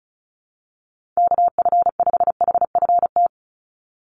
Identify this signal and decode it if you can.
Morse code, heard as a keyed tone, about 34 wpm, 700 Hz. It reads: KF5HFT
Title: USA Call Signs 35wpm